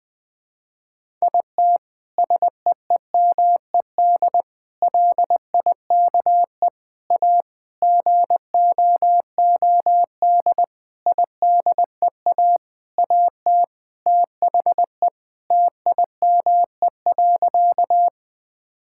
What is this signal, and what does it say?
Morse code, heard as a keyed tone, about 20 wpm, 700 Hz. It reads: IT SEEMED LIKE A GOOD IDEA AT THE TIME.
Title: Sentences with 8-Word Phrases - 20wpm